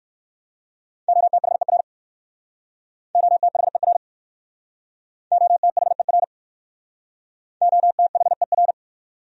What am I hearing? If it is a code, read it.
Morse code, heard as a keyed tone, about 60 wpm, 700 Hz. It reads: OTHER OTHER OTHER OTHER